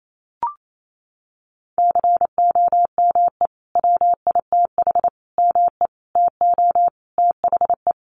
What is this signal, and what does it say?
Morse code, heard as a keyed tone, about 28 wpm, 700 Hz. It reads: COME WITH ME TO THE
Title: Sentences from Top 200 Words 28wpm